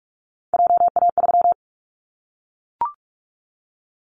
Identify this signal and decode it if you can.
Morse code, heard as a keyed tone, about 44 wpm, 700 Hz. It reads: JA3
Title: Callsign Prefixes for Top 50 DXCC - 45wpm